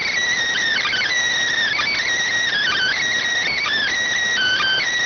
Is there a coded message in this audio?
Yes: Emergency Alert System (SAME) tones from 0.8 to 4.9 s, heard as harsh buzzing bursts.